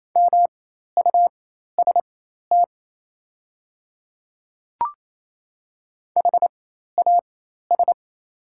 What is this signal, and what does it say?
Morse code, heard as a keyed tone, about 28 wpm, spaced as for 13 wpm, 690 Hz. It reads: MUST HAS
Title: 4x Character Spacing - Top 500 Words - 28wpm